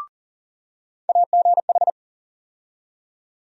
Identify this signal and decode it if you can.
Morse code, heard as a keyed tone, about 40 wpm, 700 Hz. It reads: AGH
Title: USA Call Sign Postfixes - 3 Letters 40wpm